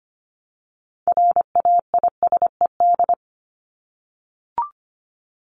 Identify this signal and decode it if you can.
Morse code, heard as a keyed tone, about 25 wpm, 700 Hz. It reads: RAISED
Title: ED Suffix - Top 100 Words with Base Words 25wpm